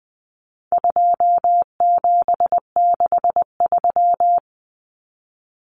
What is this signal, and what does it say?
Morse code, heard as a keyed tone, about 20 wpm, 700 Hz. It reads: 2763
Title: Common Cut Numbers - 4-Digit 20wpm